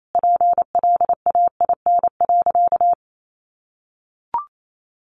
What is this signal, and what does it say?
Morse code, heard as a keyed tone, about 28 wpm, 700 Hz. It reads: PLAIN.
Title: Sentences from Top 400 Words 28wpm